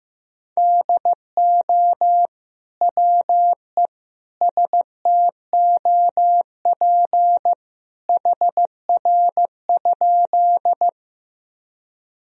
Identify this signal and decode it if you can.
Morse code, heard as a keyed tone, about 15 wpm, 700 Hz. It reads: DO WE STOP HR?
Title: Sentences from Top 300 Words and Abbreviations 15wpm